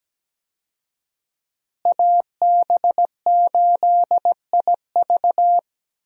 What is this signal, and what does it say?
Morse code, heard as a keyed tone, about 17 wpm, 700 Hz. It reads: AB8IV
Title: USA Call Signs - 5 Characters - Repeated 3x - 17wpm